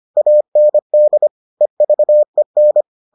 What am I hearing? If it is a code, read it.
Morse code, heard as a keyed tone, about 25 wpm, 600 Hz. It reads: AND EVEN